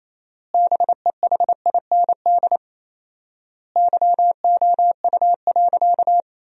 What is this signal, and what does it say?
Morse code, heard as a keyed tone, about 28 wpm, 700 Hz. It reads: BEHIND YOU.
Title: 4x Word-Spacing - Sentences from Top 500 Words - 28wpm